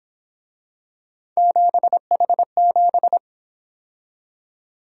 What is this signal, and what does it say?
Morse code, heard as a keyed tone, about 26 wpm, 700 Hz. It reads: 7H7
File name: Speed-Racing - Three Letter-Number Combinations - 22wpm